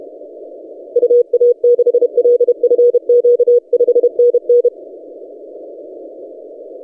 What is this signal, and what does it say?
Morse code, heard as a keyed tone, about 30 wpm, 480 Hz. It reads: UA6LFQ 5NN
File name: posts / ri0x-20cw- -0519z